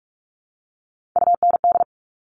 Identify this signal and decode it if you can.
Morse code, heard as a keyed tone, about 46 wpm, 700 Hz. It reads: UND
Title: Top Three Letter Combinations in Words 45wpm